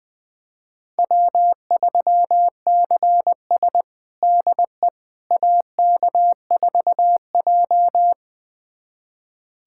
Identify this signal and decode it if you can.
Morse code, heard as a keyed tone, about 20 wpm, 700 Hz. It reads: W3CS DE AK4J
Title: USA Callsign DE Callsign - 4 Characters - 20wpm